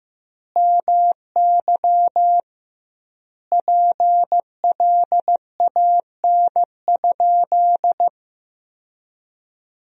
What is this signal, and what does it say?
Morse code, heard as a keyed tone, about 15 wpm, 700 Hz. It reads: MY PLAN?
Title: 2x Word-Spacing - Sentences from Top 500 Words - 15wpm